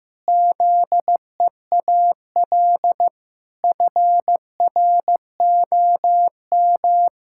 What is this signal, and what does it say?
Morse code, heard as a keyed tone, about 15 wpm, 700 Hz. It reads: ZEAL FROM
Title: Pangram Challenge v2 - 15wpm